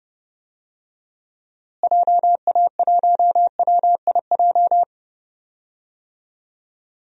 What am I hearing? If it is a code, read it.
Morse code, heard as a keyed tone, about 31 wpm, 700 Hz. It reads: JA1WIJ
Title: DXCC-100 Call Signs - 6 Characters - Repeated 2x - 30wpm